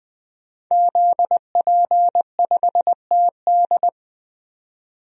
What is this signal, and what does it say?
Morse code, heard as a keyed tone, about 21 wpm, 700 Hz. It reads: ZP5TD
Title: DXCC-100 Call Signs - 5 Characters - 20wpm